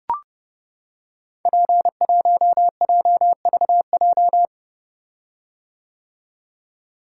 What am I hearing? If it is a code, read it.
Morse code, heard as a keyed tone, about 29 wpm, 700 Hz. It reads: P1JVJ
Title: Code Groups - 5 Letter:Numbers - 30wpm